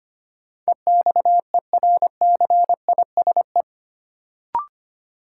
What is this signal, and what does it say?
Morse code, heard as a keyed tone, about 25 wpm, 700 Hz. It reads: EXERCISE
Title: Sentences from Top 1200 Words - New Word Drill Alt - 25wpm